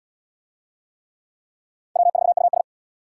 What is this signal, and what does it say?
Morse code, heard as a keyed tone, about 78 wpm, 700 Hz. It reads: W4US